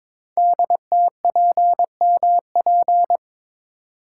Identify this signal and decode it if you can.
Morse code, heard as a keyed tone, about 23 wpm, 700 Hz. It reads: DTPMP
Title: Code Groups - 5 Letters - 22wpm